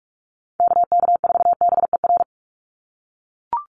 Morse code, heard as a keyed tone, about 44 wpm, 700 Hz. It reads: KK4BER
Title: USA Call Signs 45wpm